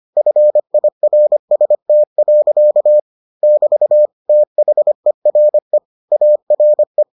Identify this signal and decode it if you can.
Morse code, heard as a keyed tone, about 25 wpm, 600 Hz. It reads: FIRST. <BT> THERE ARE